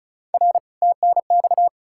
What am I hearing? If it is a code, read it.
Morse code, heard as a keyed tone, about 48 wpm, spaced as for 42 wpm, 700 Hz. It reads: R TNX
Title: Speed-Racing Common QSO Phrases Encore - 35wpm